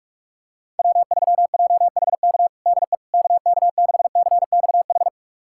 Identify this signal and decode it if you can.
Morse code, heard as a keyed tone, about 46 wpm, 700 Hz. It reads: W3JHK DE KK6CXH
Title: USA Callsign DE Callsign 45wpm